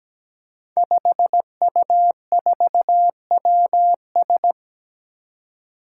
Morse code, heard as a keyed tone, about 17 wpm, 700 Hz. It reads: 5U4WS
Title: Five Challenging Letter-Number Combinations 17wpm